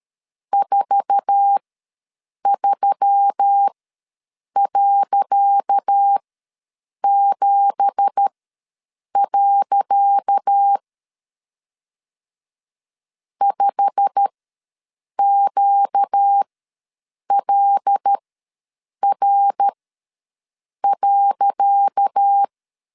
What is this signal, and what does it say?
Morse code, heard as a keyed tone, about 13 wpm, spaced as for 7 wpm, 800 Hz. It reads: 43.7. 5QLR.
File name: LEZIONI / 10 SIGNS / 10C DOT ws 12wpm 3548